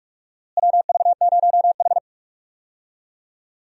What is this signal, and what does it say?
Morse code, heard as a keyed tone, about 41 wpm, 700 Hz. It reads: WV0H
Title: USA Call Signs - 4 Characters - Repeated 3x - 45wpm